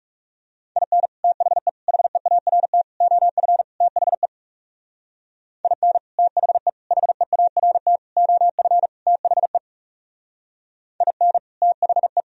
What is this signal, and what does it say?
Morse code, heard as a keyed tone, about 34 wpm, 700 Hz. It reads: IN THE HEART OF THE IN THE HEART OF THE IN THE
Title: Speed-Racing - Sets of 5 Words 30wpm